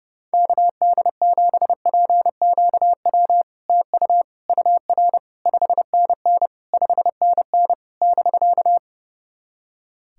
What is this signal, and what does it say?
Morse code, heard as a keyed tone, about 30 wpm, 690 Hz. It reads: KD7PQW TU UR 5NN 5NN <BK>
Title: SOTA QSO Practice 30wpm